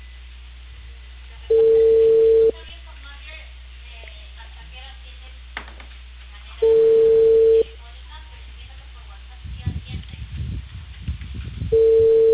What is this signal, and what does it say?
Ringback tone (China, 450 Hz), with speech.